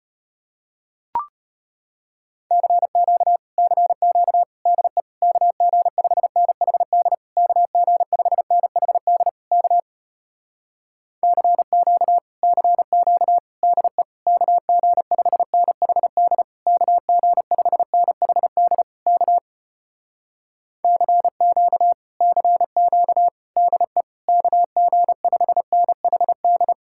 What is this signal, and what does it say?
Morse code, heard as a keyed tone, about 33 wpm, 700 Hz. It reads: CQ CQ DE KG5NHD KG5NHD K CQ CQ DE KG5NHD KG5NHD K CQ CQ DE KG5NHD